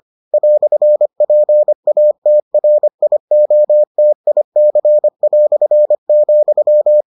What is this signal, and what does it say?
Morse code, heard as a keyed tone, about 25 wpm, 600 Hz. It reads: "PATRIOTIC",